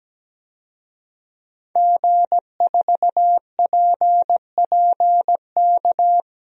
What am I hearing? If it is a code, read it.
Morse code, heard as a keyed tone, about 17 wpm, 700 Hz. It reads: G4PPK
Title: DXCC-100 Call Signs - 5 Characters - Repeated 2x - 17wpm